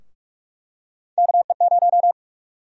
Morse code, heard as a keyed tone, about 44 wpm, 700 Hz. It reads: KE0
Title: USA Call Sign Prefixes - 45wpm